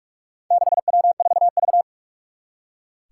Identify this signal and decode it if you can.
Morse code, heard as a keyed tone, about 44 wpm, 700 Hz. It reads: 6W4V
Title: Four Challenging Letter-Number Combinations 45wpm